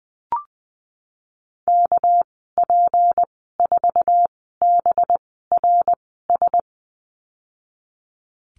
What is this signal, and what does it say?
Morse code, heard as a keyed tone, about 19 wpm, spaced as for 14 wpm, 700 Hz. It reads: KP4BRS
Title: 245 Lesson 39g - Letter z - Call Signs 20wpm